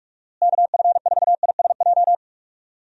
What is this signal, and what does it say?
Morse code, heard as a keyed tone, about 46 wpm, 700 Hz. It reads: KF4ISJ
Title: USA Call Signs - 6 Characters - Repeated 3x - 45wpm